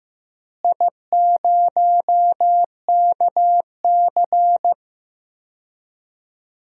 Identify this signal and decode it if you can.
Morse code, heard as a keyed tone, about 15 wpm, 700 Hz. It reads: I0KC
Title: DXCC-100 Call Signs - 4 Characters - Repeated 2x - 15wpm